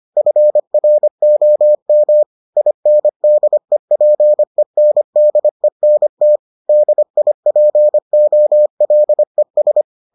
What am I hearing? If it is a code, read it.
Morse code, heard as a keyed tone, about 25 wpm, 600 Hz. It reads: FROM INDEPENDENT DIPOLES